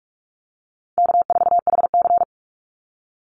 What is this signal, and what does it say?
Morse code, heard as a keyed tone, about 44 wpm, 700 Hz. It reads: K4HC
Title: Easy Call Signs Repeated 3x - 45wpm